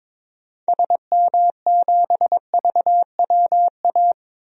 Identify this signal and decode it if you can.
Morse code, heard as a keyed tone, about 21 wpm, 700 Hz. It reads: SM7VWA
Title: DXCC-100 Call Signs - 6 Characters - 22wpm